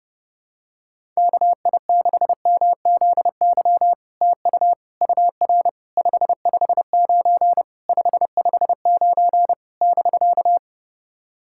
Morse code, heard as a keyed tone, about 30 wpm, 700 Hz. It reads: KI6MZY TU UR 559 559 <BK>
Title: SOTA QSO Practice 30wpm